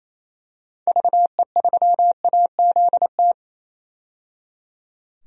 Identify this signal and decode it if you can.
Morse code, heard as a keyed tone, about 28 wpm, 700 Hz. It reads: VE3AZT
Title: DXCC-100 Call Signs - 6 Characters - 28wpm